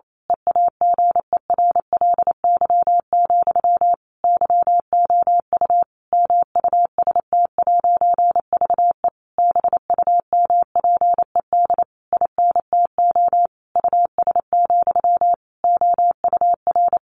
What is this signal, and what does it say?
Morse code, heard as a keyed tone, about 28 wpm, 700 Hz. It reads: EAGERLY, YOU MUST'VE BUMPED INTO US, OUR
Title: A Sound Of Thunder 28wpm